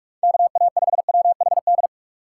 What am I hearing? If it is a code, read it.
Morse code, heard as a keyed tone, about 43 wpm, 700 Hz. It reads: KA5WHD
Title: USA Call Signs - 6 Characters - 2x Character Spacing - 45wpm